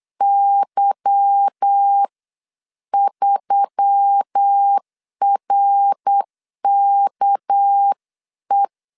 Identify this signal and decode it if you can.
Morse code, heard as a keyed tone, about 8 wpm, spaced as for 7 wpm, 800 Hz. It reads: Y3RKE
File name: 8wpm